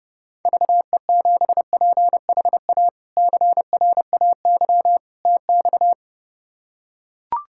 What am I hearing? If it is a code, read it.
Morse code, heard as a keyed tone, about 30 wpm, 700 Hz. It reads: VE7PHA CRAY TX
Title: K1USN SST QSO Practice - 30wpm